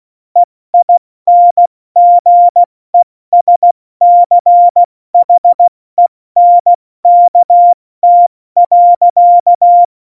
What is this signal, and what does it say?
Morse code, heard as a keyed tone, about 16 wpm, 700 Hz. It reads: EINGESCHENKT.
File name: Morsekurs / Witze / Witz 22. Oktober 2019, 70 F80